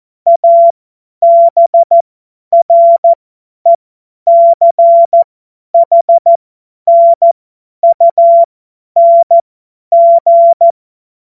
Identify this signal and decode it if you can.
Morse code, heard as a keyed tone, about 14 wpm, spaced as for 10 wpm, 680 Hz. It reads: ABRECHNUNG
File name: Morsekurs / Witze / Witz 23. August 2022 50 F70